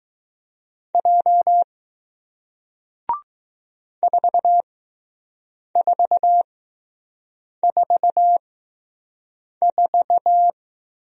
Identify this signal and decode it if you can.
Morse code, heard as a keyed tone, about 20 wpm, spaced as for 6 wpm, 700 Hz. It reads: J 4444